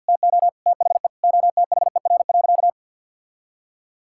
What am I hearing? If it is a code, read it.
Morse code, heard as a keyed tone, about 50 wpm, 700 Hz. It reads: TO THE OTHER.
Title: Sentences from Top 1000 Words - Review Part 5 - 50wpm